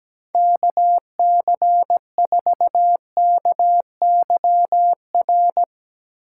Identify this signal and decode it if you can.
Morse code, heard as a keyed tone, about 17 wpm, 700 Hz. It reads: KC4KYR
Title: Speed Racing - USA 2x3 Call Signs 17wpm